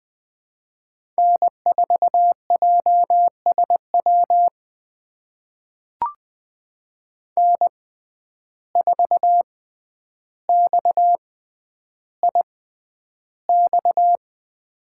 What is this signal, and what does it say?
Morse code, heard as a keyed tone, about 20 wpm, spaced as for 10 wpm, 700 Hz. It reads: N4JSW N4XIX
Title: USA Call Signs - 5 Characters - 6x Character Spacing - 20wpm